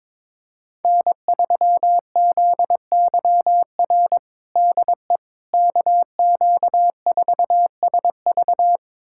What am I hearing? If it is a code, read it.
Morse code, heard as a keyed tone, about 22 wpm, 700 Hz. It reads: N3ZYR DE KQ4SV
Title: USA Callsign DE Callsign - 5 Characters - 22wpm